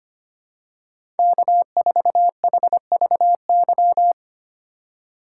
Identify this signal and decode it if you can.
Morse code, heard as a keyed tone, about 25 wpm, 700 Hz. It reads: K4HVY
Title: USA Call Signs - 5 Characters - 3x Character Spacing - 25wpm